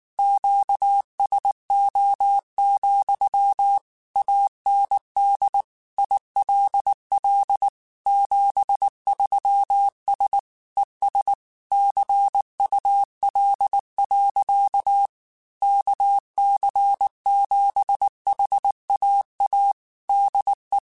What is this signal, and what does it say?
Morse code, heard as a keyed tone, about 19 wpm, 800 Hz. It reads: QSO, AND ILL 73S ES CUL. KC7HAA DE